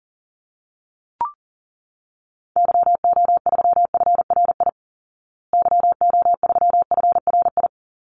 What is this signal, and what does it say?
Morse code, heard as a keyed tone, about 40 wpm, 700 Hz. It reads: YO3FRI YO3FRI
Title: Call Signs Repeated 3x - 40wpm